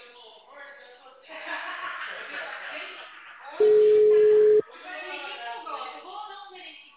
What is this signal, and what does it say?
Ringback tone (Europe, 425 Hz), with speech.